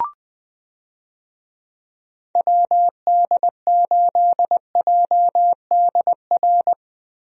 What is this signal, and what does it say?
Morse code, heard as a keyed tone, about 20 wpm, 700 Hz. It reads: WD8JDR
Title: USA Call Signs - 6 Characters - Repeated 3x - 20wpm